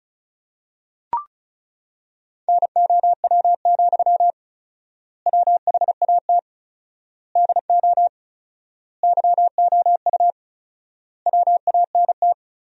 Morse code, heard as a keyed tone, about 36 wpm, 700 Hz. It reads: NOW, WHAT DO YOU WANT
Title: 4x Word-Spacing - Sentences from Top 200 Words - 35wpm